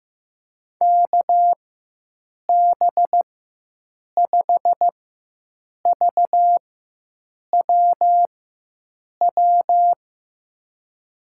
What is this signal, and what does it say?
Morse code, heard as a keyed tone, about 15 wpm, spaced as for 7 wpm, 700 Hz. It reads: KB5VWW
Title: USA Call Signs - 6 Characters - 4x Character Spacing - 15wpm